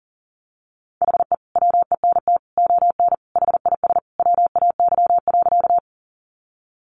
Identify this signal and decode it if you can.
Morse code, heard as a keyed tone, about 40 wpm, 700 Hz. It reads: HE WENT ON HIS WAY.
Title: Sentences from Top 200 Words 40wpm